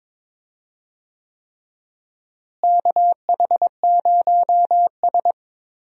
Morse code, heard as a keyed tone, about 22 wpm, 700 Hz. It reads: KH0S
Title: USA Call Signs - 4 Characters - Repeated 3x - 22wpm